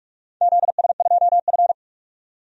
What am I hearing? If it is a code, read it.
Morse code, heard as a keyed tone, about 46 wpm, 700 Hz. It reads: ZS2F
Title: DXCC-100 Call Signs - 4 Characters - 45wpm